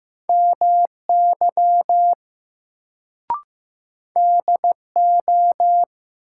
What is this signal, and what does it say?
Morse code, heard as a keyed tone, about 15 wpm, 700 Hz. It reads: MY DO
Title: Sets of 2 Words - New Word Drill - 15wpm